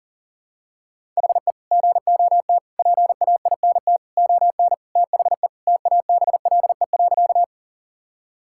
Morse code, heard as a keyed tone, about 41 wpm, 700 Hz. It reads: HE GOT PAINT ON THE TABLE.
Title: Sentences from Top 1000 Words - Review Part 6 - 40wpm